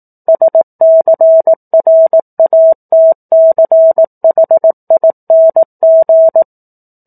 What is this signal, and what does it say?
Morse code, heard as a keyed tone, about 18 wpm, 650 Hz. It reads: SCRATCHING
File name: 04-Chapter-04 18 WPM